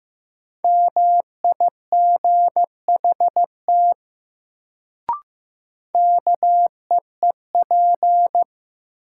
Morse code, heard as a keyed tone, about 15 wpm, 700 Hz. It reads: MIGHT KEEP
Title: Sets of 2 Words - New Word Drill - 15wpm